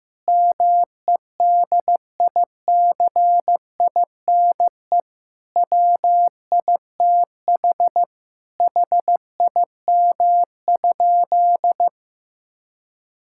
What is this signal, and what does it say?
Morse code, heard as a keyed tone, about 16 wpm, 700 Hz. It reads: MEDICINE WITH HIM?
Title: Sentences from Top 1000 Words - Review Part 6 - 15wpm